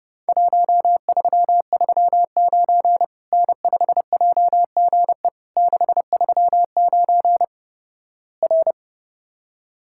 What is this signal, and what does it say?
Morse code, heard as a keyed tone, about 30 wpm, 700 Hz. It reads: 1339 N5JGE 639 R
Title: Worked All Europe DX Contest - QTC Exchange - 30wpm